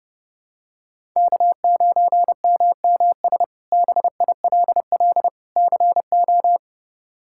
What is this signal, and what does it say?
Morse code, heard as a keyed tone, about 31 wpm, 700 Hz. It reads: K9MMS BILL CO
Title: K1USN SST QSO Practice - 30wpm